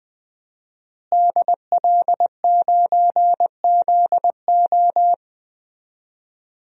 Morse code, heard as a keyed tone, about 20 wpm, 700 Hz. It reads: DL9ZO
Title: DXCC-100 Call Signs - 5 Characters - Repeated 2x - 20wpm